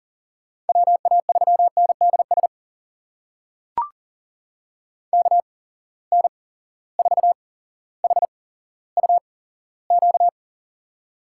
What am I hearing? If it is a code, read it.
Morse code, heard as a keyed tone, about 40 wpm, spaced as for 17 wpm, 700 Hz. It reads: WA3NDS KN4HUQ
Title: USA Call Signs - 6 Characters - 8x Character Spacing - 40wpm